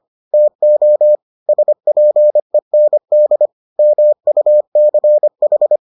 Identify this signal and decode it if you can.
Morse code, heard as a keyed tone, about 25 wpm, 600 Hz. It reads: TO SPEND MUCH